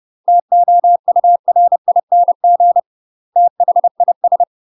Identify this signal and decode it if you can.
Morse code, heard as a keyed tone, about 30 wpm, 700 Hz. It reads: TOURING THIS
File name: UK MASTER QSOs TRAINING / 30WPM / UK MASTER QSOs 0051